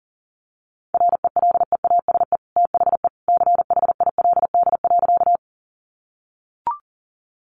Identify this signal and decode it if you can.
Morse code, heard as a keyed tone, about 40 wpm, 700 Hz. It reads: RELEASE THE CHILD.
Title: Sentences from Top 1100 Words 40wpm